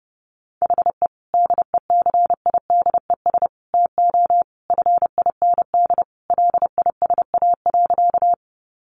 Morse code, heard as a keyed tone, about 30 wpm, 700 Hz. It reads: HE DECIDES TO FIND LISA.
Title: Additional Easy Phrases 30wpm